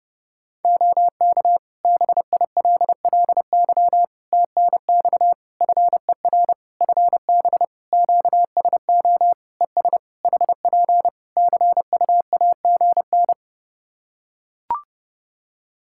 Morse code, heard as a keyed tone, about 29 wpm, 700 Hz. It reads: OK BILLY TNX FER FB QSO ES HP CUAGN
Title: Practice QSO - Exchange 4 - Ending - 30wpm